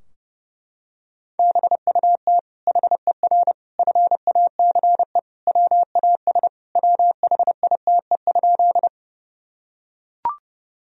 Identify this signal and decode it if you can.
Morse code, heard as a keyed tone, about 32 wpm, 700 Hz. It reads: BUT HER FACE WAS WHITE?
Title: Sentences from Top 1000 Words - Review Part 2 - 30wpm